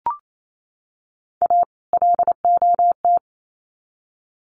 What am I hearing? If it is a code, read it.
Morse code, heard as a keyed tone, about 28 wpm, 700 Hz. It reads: A LOT
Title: Sets of 2 Words 28wpm